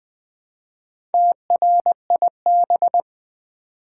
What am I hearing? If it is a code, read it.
Morse code, heard as a keyed tone, about 20 wpm, 700 Hz. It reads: TRIB
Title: Build A Word - 1 Letter At A Time - 5-Letter Words - 20wpm